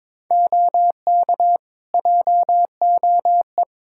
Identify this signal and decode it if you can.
Morse code, heard as a keyed tone, about 21 wpm, 700 Hz. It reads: OK JOE
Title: Practice QSO - Exchange 3 - More Info - 22wpm